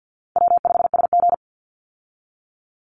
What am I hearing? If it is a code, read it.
Morse code, heard as a keyed tone, about 50 wpm, 700 Hz. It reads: W5SG
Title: Easy Call Signs Repeated 2x - 50wpm